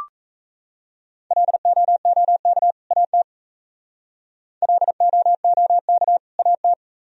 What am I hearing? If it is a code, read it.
Morse code, heard as a keyed tone, about 35 wpm, 700 Hz. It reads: LOOK AT LOOK AT